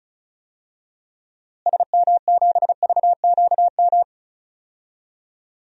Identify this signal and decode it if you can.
Morse code, heard as a keyed tone, about 34 wpm, 700 Hz. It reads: SM7VQM
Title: DXCC-100 Call Signs - 6 Characters - Repeated 2x - 35wpm